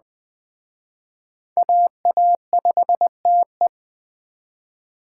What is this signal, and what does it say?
Morse code, heard as a keyed tone, about 22 wpm, 700 Hz. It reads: AA5TE